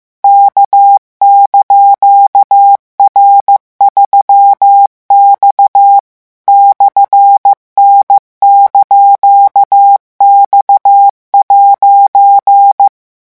Morse code, heard as a keyed tone, about 15 wpm, 800 Hz. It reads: K)R3X /N)X'
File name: morse mp3 / revision17-vitesse 14 mots